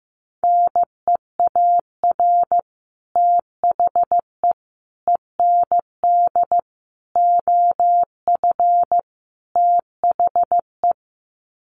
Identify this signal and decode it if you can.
Morse code, heard as a keyed tone, about 15 wpm, 700 Hz. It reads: NEAR THE END OF THE
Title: Sets of 5 Words Encore 15wpm